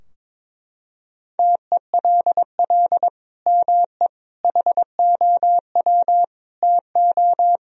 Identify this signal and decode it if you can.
Morse code, heard as a keyed tone, about 22 wpm, 700 Hz. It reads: TELL ME HOW TO
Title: Sentences from Top 1000 Words - Review Part 2 - 22wpm